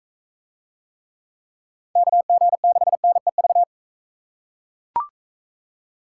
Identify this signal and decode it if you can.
Morse code, heard as a keyed tone, about 42 wpm, 700 Hz. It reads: KG6NEV